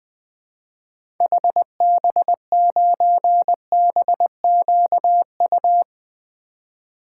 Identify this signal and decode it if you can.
Morse code, heard as a keyed tone, about 20 wpm, 700 Hz. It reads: HB9BQU
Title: 2023 SOTA Activator Call Signs - 20wpm